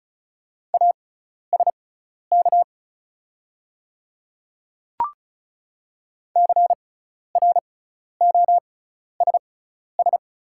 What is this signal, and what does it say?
Morse code, heard as a keyed tone, about 36 wpm, spaced as for 12 wpm, 690 Hz. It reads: ASK CROSS
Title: 6x Character Spacing - Top 300 Words - 35wpm